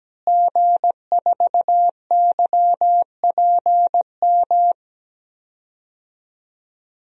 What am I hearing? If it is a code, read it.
Morse code, heard as a keyed tone, about 17 wpm, 700 Hz. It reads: G4YPM
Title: DXCC-100 Call Signs - 5 Characters - Repeated 2x - 17wpm